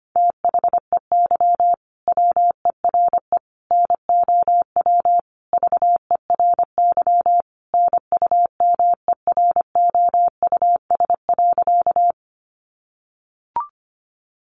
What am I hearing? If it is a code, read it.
Morse code, heard as a keyed tone, about 25 wpm, 700 Hz. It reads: THEY WERE NOW VERY NUMEROUS.
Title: Sentences from Top 1400 Words 25wpm